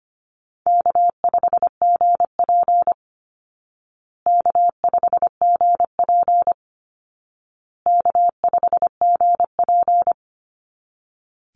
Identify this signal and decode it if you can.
Morse code, heard as a keyed tone, about 25 wpm, 700 Hz. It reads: K5GP K5GP K5GP
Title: Call Signs Repeated 2x - 25wpm